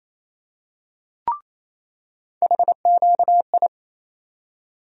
Morse code, heard as a keyed tone, about 27 wpm, 700 Hz. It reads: HQI
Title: USA Call Sign Postfixes - 3 Letters 28wpm